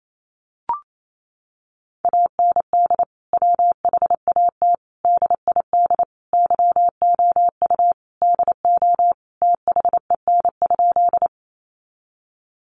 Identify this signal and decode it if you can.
Morse code, heard as a keyed tone, about 28 wpm, 700 Hz. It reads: AND WHAT DID YOU DO THEN?
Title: Sentences from Top 100 Words 28wpm